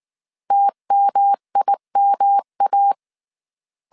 Morse code, heard as a keyed tone, about 19 wpm, 800 Hz. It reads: TMIMA
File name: CW / DET / 1 3 1500C S B 18wpm 1916